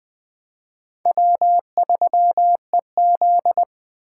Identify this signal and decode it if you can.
Morse code, heard as a keyed tone, about 20 wpm, 700 Hz. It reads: W3EZ
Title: Speed Racing - USA Call Signs 17wpm